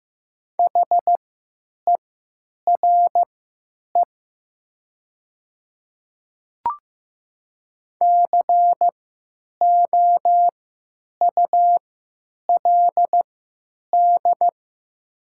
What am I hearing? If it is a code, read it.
Morse code, heard as a keyed tone, about 15 wpm, spaced as for 9 wpm, 700 Hz. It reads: HERE COULD
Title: 3x Character Spacing - Top 200 Words - 15wpm